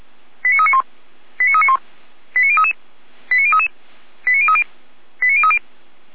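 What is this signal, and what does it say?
Five-tone selective call (ZVEI), heard as quick runs of five tones, tones of about 70 ms. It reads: 89391 89391 89033 89033 89030 89030
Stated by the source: ZVEI five-tone selective call 89050 received by SDR.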